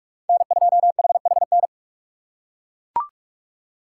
Morse code, heard as a keyed tone, about 40 wpm, 700 Hz. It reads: N2HHN